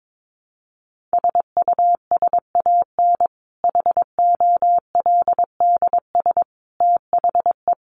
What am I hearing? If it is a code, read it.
Morse code, heard as a keyed tone, about 22 wpm, 700 Hz. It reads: SUSAN HOLDS THE
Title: Easy Phrases 22wpm